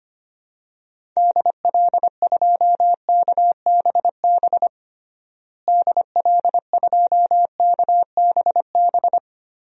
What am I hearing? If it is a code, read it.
Morse code, heard as a keyed tone, about 25 wpm, 700 Hz. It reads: DL2KBB DL2KBB
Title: DXCC-100 Call Signs - 6 Characters - Repeated 3x - 25wpm